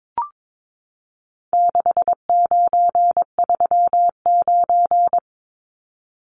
Morse code, heard as a keyed tone, about 22 wpm, 700 Hz. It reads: 6939
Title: Numbers - 4-Digit 22wpm